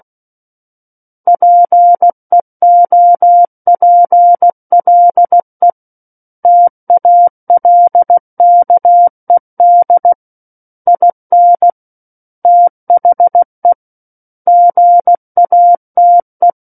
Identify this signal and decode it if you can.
Morse code, heard as a keyed tone, about 16 wpm, 700 Hz. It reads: PEOPLE TALKED IN THE GATE
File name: Book One / 07-Chapter 7 - 16 WPM